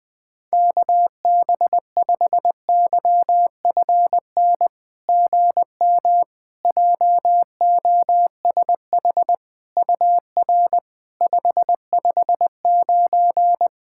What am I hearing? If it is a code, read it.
Morse code, heard as a keyed tone, about 21 wpm, 700 Hz. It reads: KB5YFN GM JOSH UR 559
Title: SOTA QSO Practice 20wpm